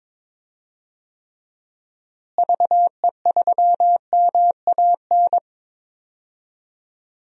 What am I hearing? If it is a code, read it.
Morse code, heard as a keyed tone, about 22 wpm, 700 Hz. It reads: VE3MAN
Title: DXCC-100 Call Signs - 6 Characters - Repeated 2x - 22wpm